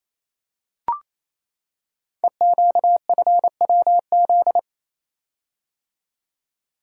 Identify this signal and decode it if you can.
Morse code, heard as a keyed tone, about 28 wpm, 690 Hz. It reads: EQFWZ
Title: Code Groups - 5 Letter:Numbers - 28wpm